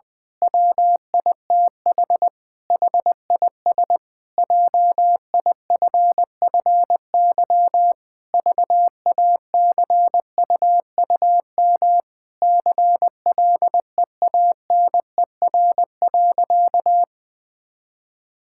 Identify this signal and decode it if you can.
Morse code, heard as a keyed tone, about 21 wpm, 700 Hz. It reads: WITH HIS JIFFY VACUUM CLEANER.
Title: Pangram Challenge v2 - 20wpm